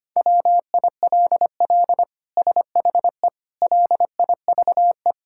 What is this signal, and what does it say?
Morse code, heard as a keyed tone, about 25 wpm, 700 Hz. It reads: WILL SHE LIVE
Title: Sentences from Top 1000 Words - Review Part 6 - 25wpm